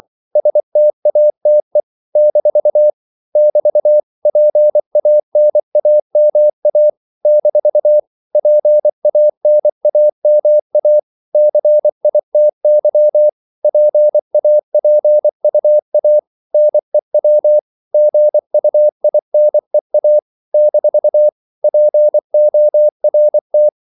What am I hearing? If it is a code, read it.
Morse code, heard as a keyed tone, about 24 wpm, 600 Hz. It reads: STATE - <BT> PANAMA - PANAMA CITY PAPUA NEW GUINEA - PORT
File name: CW / Practice / NationalCapitals24